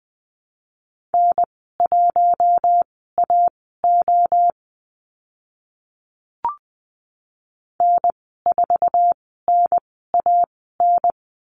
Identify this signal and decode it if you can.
Morse code, heard as a keyed tone, about 20 wpm, spaced as for 14 wpm, 700 Hz. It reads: N1AO N4NAN
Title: 062 Lesson 11g - Letter h - Call Signs 20wpm